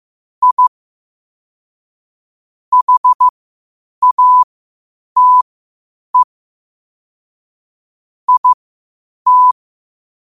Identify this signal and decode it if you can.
Morse code, heard as a keyed tone, about 15 wpm, spaced as for 8 wpm, 1000 Hz. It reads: I HATE IT
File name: qotdfiles / 8wpm